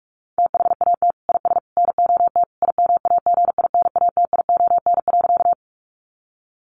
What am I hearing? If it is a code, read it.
Morse code, heard as a keyed tone, about 44 wpm, 700 Hz. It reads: THAT IS NOT IMAGINATION.
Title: Sentences from Top 1300 Words 45wpm